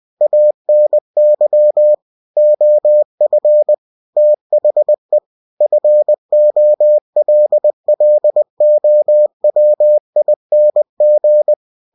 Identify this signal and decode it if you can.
Morse code, heard as a keyed tone, about 20 wpm, 600 Hz. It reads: ANY OF THE FOLLOWING
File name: cw-books / winnie-the-pooh / winnie-20wpm-page-0120